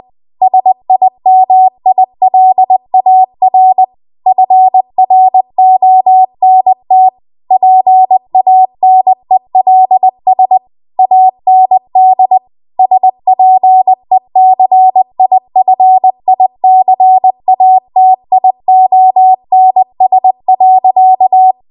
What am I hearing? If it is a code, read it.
Morse code, heard as a keyed tone, about 20 wpm, 750 Hz. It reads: SIMILAR FRONT PANELS AND SPECIFICATIONS.
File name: Archive / 20 WPM / 20WPM